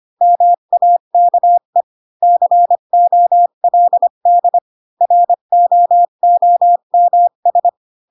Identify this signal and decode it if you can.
Morse code, heard as a keyed tone, about 25 wpm, 700 Hz. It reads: MAKE COLD ROOMS
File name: Pinocchio / Pinocchio ch1 20 25 ik1hge